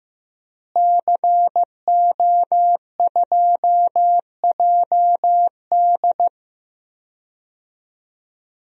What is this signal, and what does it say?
Morse code, heard as a keyed tone, about 15 wpm, 700 Hz. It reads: CO2JD
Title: DXCC-100 Call Signs - 5 Characters - Repeated 3x - 15wpm